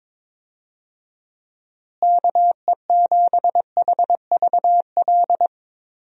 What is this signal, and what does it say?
Morse code, heard as a keyed tone, about 22 wpm, 700 Hz. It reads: KE7HVL
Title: USA Call Signs - 6 Characters - Repeated 2x - 22wpm